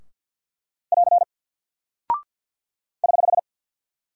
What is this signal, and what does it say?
Morse code, heard as a keyed tone, about 48 wpm, 700 Hz. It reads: <AR> <HH>
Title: Single Character (Letter:Number:Punctuation:Prosign) Alt 50wpm